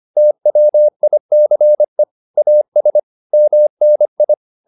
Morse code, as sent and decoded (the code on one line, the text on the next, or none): - .-- .. -.-. / .- ... / -- -. ..
TWICE AS MNI